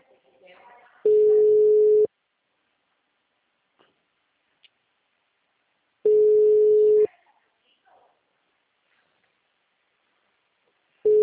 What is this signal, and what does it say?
Ringback tone (Europe, 425 Hz), with speech.